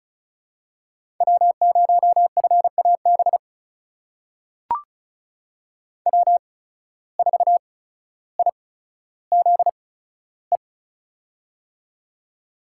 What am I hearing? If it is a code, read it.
Morse code, heard as a keyed tone, about 36 wpm, spaced as for 15 wpm, 700 Hz. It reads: W0FAB W4IZE
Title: USA Call Signs - 5 Characters - 8x Character Spacing - 35wpm